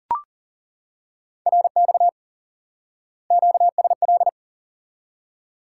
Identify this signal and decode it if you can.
Morse code, heard as a keyed tone, about 40 wpm, 700 Hz. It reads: RX QSL
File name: IQE Course 3 - Lesson 89 - YAGI - Two Elements - 40wpm